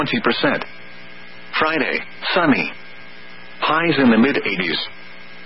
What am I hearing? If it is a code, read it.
Emergency Alert System (SAME) tones, heard as harsh buzzing bursts, with speech. It is heard from 0.8 to 4.9 s.